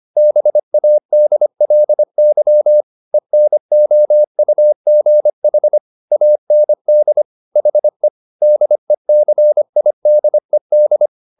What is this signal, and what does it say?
Morse code, heard as a keyed tone, about 25 wpm, 600 Hz. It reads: BADLY ENOUGH AND HE DECIDED